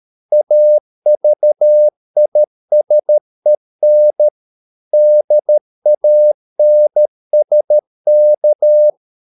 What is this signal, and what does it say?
Morse code, heard as a keyed tone, about 13 wpm, 600 Hz. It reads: AVISEN DANSK
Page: Ferie 13 wpm